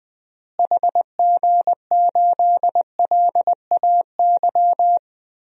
Morse code, heard as a keyed tone, about 20 wpm, 700 Hz. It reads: HG8LAY
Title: DXCC-100 Call Signs - 6 Characters - Repeated 2x - 20wpm